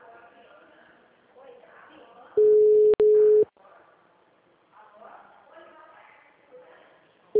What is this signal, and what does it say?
Ringback tone (Europe, 425 Hz), with speech.